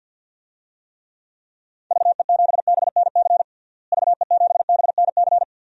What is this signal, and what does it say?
Morse code, heard as a keyed tone, about 49 wpm, 700 Hz. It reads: VE7BNC VE7BNC
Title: DXCC-100 Call Signs - 6 Characters - Repeated 2x - 50wpm